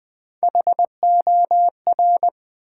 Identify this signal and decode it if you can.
Morse code, heard as a keyed tone, about 20 wpm, 700 Hz. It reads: HOR
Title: Build A Word - 1 Letter At A Time - 5-Letter Words - 20wpm